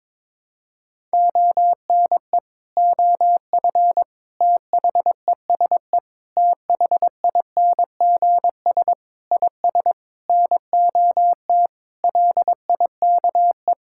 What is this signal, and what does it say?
Morse code, heard as a keyed tone, about 22 wpm, 700 Hz. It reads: ONE OF THESE THINGS IS NOT LIKE
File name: practice / 22wpm / Sentences with 8-Word Phrases - 22wpm